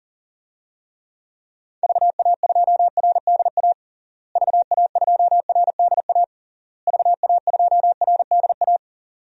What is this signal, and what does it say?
Morse code, heard as a keyed tone, about 40 wpm, 700 Hz. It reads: VA2RDA VA2RDA VA2RDA
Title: DXCC-100 Call Signs - 6 Characters - Repeated 3x - 40wpm